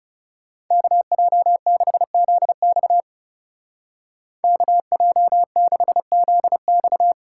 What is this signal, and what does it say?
Morse code, heard as a keyed tone, about 37 wpm, 700 Hz. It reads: KJ6ZX KJ6ZX
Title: Speed Racing - USA 2x2 Call Signs 30wpm